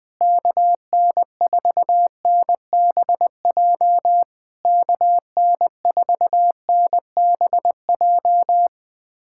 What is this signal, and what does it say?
Morse code, heard as a keyed tone, about 21 wpm, 700 Hz. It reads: KN4NBJ KN4NBJ
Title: POTA QSO Practice - 20wpm